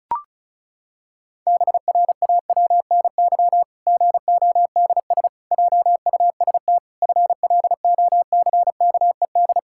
Morse code, heard as a keyed tone, about 34 wpm, 700 Hz. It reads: BRAWNY GODS JUST FLOCKED
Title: Pangram Challenge v2 - 35wpm